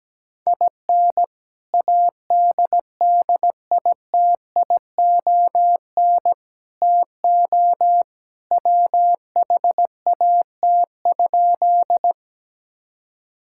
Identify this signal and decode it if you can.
Morse code, heard as a keyed tone, about 17 wpm, 700 Hz. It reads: IN ADDITION TO WHAT?
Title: Sentences from Top 1000 Words - Review Part 6 - 17wpm